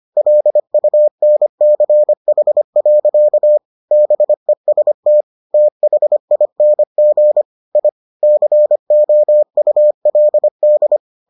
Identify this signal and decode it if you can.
Morse code, heard as a keyed tone, about 25 wpm, 600 Hz. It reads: LUNCH. BEST THING I COULD